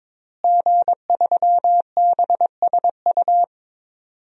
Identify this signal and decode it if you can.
Morse code, heard as a keyed tone, about 22 wpm, 700 Hz. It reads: G3BSU
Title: DXCC-100 Call Signs - 5 Characters - Repeated 3x - 22wpm